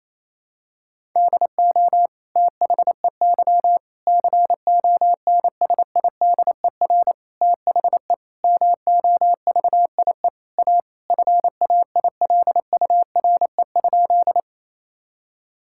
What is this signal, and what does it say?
Morse code, heard as a keyed tone, about 28 wpm, 700 Hz. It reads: DO THEY CONSIDER THE MOVIE A FAILURE?
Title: Sentences from Top 1000 Words - Review Part 6 - 28wpm